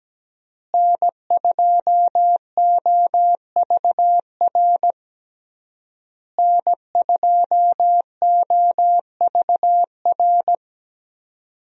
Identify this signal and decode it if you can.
Morse code, heard as a keyed tone, about 17 wpm, 700 Hz. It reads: N2OVR N2OVR
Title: USA Call Signs - 5 Characters - Repeated 3x - 17wpm